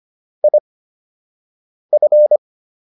Morse code, heard as a keyed tone, about 25 wpm, spaced as for 6 wpm, 600 Hz. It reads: IF